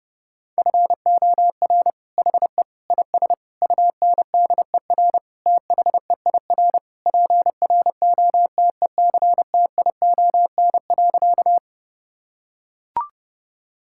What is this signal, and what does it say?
Morse code, heard as a keyed tone, about 30 wpm, 700 Hz. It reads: FOR HE IS UNDER THEIR PROTECTION.
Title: Sentences from Top 1600 Words 30wpm